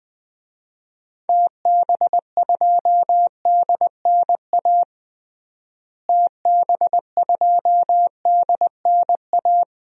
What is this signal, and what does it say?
Morse code, heard as a keyed tone, about 20 wpm, 700 Hz. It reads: TB2DNA TB2DNA
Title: DXCC-100 Call Signs - 6 Characters - Repeated 2x - 20wpm